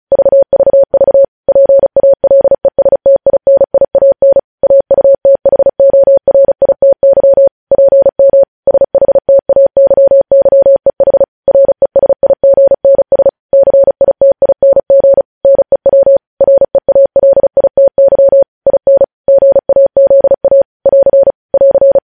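Morse code, heard as a keyed tone, about 36 wpm, 550 Hz. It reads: VVV PALESTINIAN AUTHORITY PM SHTAYYEH RESIGNS CITING NEW REALITY IN GAZA <AR> <AR>
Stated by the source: News headlines sent in Morse at 35 WPM.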